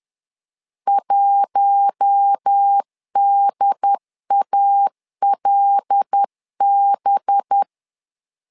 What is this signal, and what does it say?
Morse code, heard as a keyed tone, about 10 wpm, 800 Hz. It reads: 1DALB
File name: LEZIONI / 5 LBVC / 5 LBVC 10wpm 2349